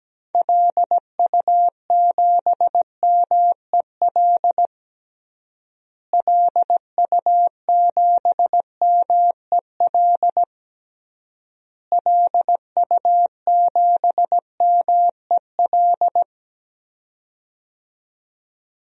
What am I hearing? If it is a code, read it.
Morse code, heard as a keyed tone, about 17 wpm, 700 Hz. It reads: LU7MEL LU7MEL LU7MEL
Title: DXCC-100 Call Signs - 6 Characters - Repeated 3x - 17wpm